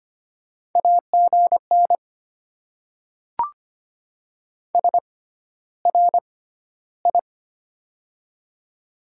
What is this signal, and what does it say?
Morse code, heard as a keyed tone, about 25 wpm, spaced as for 13 wpm, 700 Hz. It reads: AGN SRI
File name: Top 90 QSO Elements - 6x Character Spacing - 25wpm